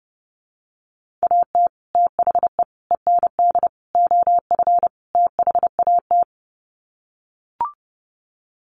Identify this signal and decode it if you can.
Morse code, heard as a keyed tone, about 30 wpm, 700 Hz. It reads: AT THE END OF THAT
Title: Sets of 5 Words Encore2 30wpm